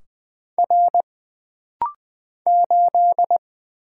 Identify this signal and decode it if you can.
Morse code, heard as a keyed tone, about 20 wpm, 700 Hz. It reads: R 8
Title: Single Letter-Number - Rapid-Fire 20wpm